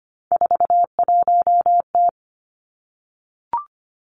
Morse code, heard as a keyed tone, about 25 wpm, 700 Hz. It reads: 41T
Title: Three Challenging Letter-Number Combinations 25wpm